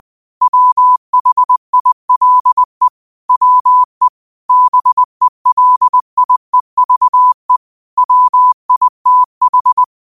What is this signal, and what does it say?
Morse code, heard as a keyed tone, about 20 wpm, 1000 Hz. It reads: WHILE WE BELIEVE WITH